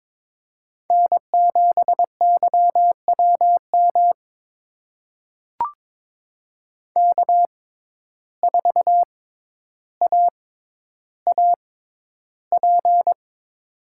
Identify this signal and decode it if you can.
Morse code, heard as a keyed tone, about 22 wpm, spaced as for 11 wpm, 700 Hz. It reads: N7YWM K4AAP
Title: USA Call Signs - 5 Characters - 6x Character Spacing - 22wpm